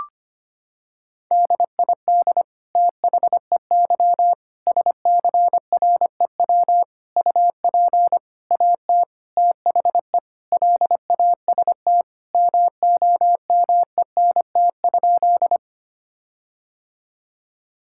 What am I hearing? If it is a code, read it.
Morse code, heard as a keyed tone, about 25 wpm, 700 Hz. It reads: DID THEY SCREW UP AT THE LAST MOMENT?
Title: Sentences from Top 1600 Words 25wpm